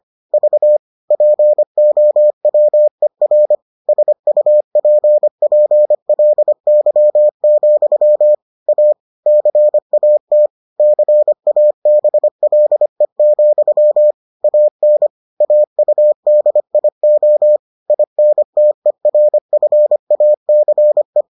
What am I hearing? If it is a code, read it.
Morse code, heard as a keyed tone, about 25 wpm, 600 Hz. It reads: V POWER SUPPLY, A CAT CABLE, AN AUDIO INTERFACE